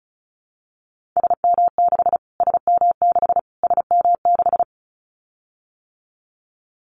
Morse code, heard as a keyed tone, about 34 wpm, 700 Hz. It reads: SM6 SM6 SM6
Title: Callsign Prefixes for Top 25 DXCC - Repeated 3x - 35wpm